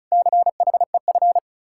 Morse code, heard as a keyed tone, about 45 wpm, 700 Hz. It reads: CHEF